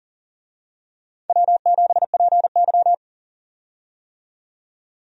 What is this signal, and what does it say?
Morse code, heard as a keyed tone, about 39 wpm, 700 Hz. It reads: W7PY
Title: USA Call Signs - 4 Characters - 40wpm